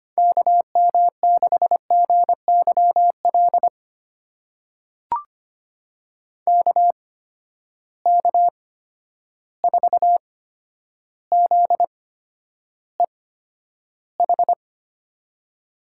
Morse code, heard as a keyed tone, about 25 wpm, spaced as for 11 wpm, 700 Hz. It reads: KM6GYL KK4ZEH
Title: USA Call Signs - 6 Characters - 8x Character Spacing - 25wpm